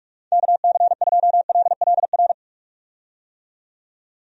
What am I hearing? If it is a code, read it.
Morse code, heard as a keyed tone, about 46 wpm, 700 Hz. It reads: KC2LLR
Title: USA Call Signs - 6 Characters - 45wpm